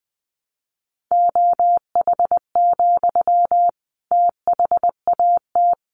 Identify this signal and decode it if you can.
Morse code, heard as a keyed tone, about 20 wpm, 700 Hz. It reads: OH, THAT
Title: Sentences from Top 1100 Words 20wpm